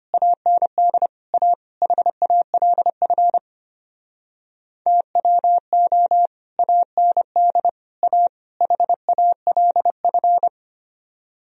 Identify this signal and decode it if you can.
Morse code, heard as a keyed tone, about 32 wpm, 700 Hz. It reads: AND A HALF TWO AND A HALF